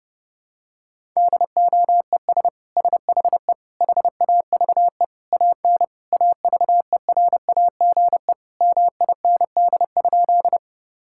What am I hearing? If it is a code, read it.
Morse code, heard as a keyed tone, about 30 wpm, 700 Hz. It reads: DOES SHE HAVE AN AVERAGE MIND?
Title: Sentences from Top 1000 Words - Review Part 6 - 30wpm